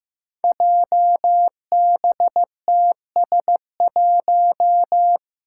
Morse code, heard as a keyed tone, about 15 wpm, 700 Hz. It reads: JBTS1
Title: Five Challenging Letter-Number Combinations 15wpm